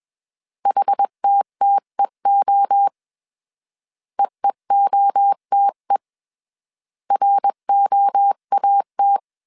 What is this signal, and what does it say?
Morse code, heard as a keyed tone, about 21 wpm, 800 Hz. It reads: HTTEO EEOTE ROAT